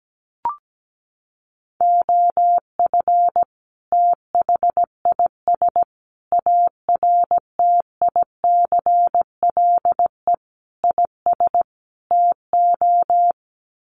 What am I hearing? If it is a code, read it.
Morse code, heard as a keyed tone, about 17 wpm, 700 Hz. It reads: OF THIS ARTICLE IS TO
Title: Sets of 5 Words Encore2 17wpm